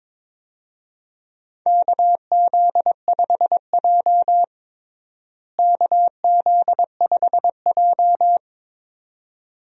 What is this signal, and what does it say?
Morse code, heard as a keyed tone, about 22 wpm, 700 Hz. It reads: KZ5J KZ5J
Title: USA Call Signs - 4 Characters - Repeated 2x - 22wpm